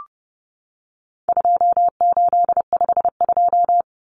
Morse code, heard as a keyed tone, about 30 wpm, 700 Hz. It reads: 2852
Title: Common Cut Numbers - 4-Digit 30wpm